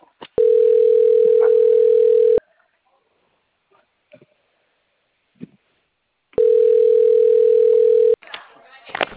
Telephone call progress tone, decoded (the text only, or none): ringback tone